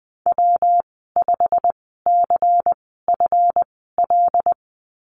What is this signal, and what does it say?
Morse code, heard as a keyed tone, about 20 wpm, spaced as for 15 wpm, 700 Hz. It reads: W5CFL
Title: 140 Lesson 23g - Letter f - Call Signs 20wpm